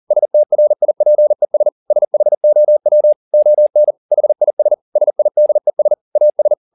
Morse code, heard as a keyed tone, about 40 wpm, 600 Hz. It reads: STRIPES SHOW ON HIS SIDES AS
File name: the old man and the sea 2 40wpm